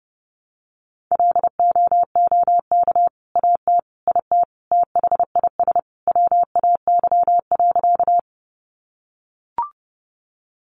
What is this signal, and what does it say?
Morse code, heard as a keyed tone, about 30 wpm, 700 Hz. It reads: LOOK AT IT THIS WAY.
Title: Sentences from Top 100 Words 30wpm